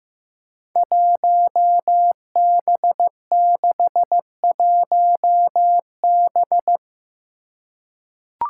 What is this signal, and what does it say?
Morse code, heard as a keyed tone, about 16 wpm, 700 Hz. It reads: 1B61B
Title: Five Challenging Letter-Number Combinations 15wpm